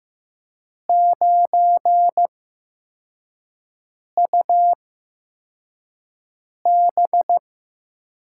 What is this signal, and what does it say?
Morse code, heard as a keyed tone, about 15 wpm, spaced as for 4 wpm, 700 Hz. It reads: 9UB
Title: USA Call Signs - 5 Characters - 8x Character Spacing - 15wpm